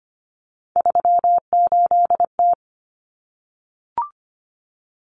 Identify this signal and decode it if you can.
Morse code, heard as a keyed tone, about 25 wpm, 700 Hz. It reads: 38T
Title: Common Cut Numbers - 3-Digit 25wpm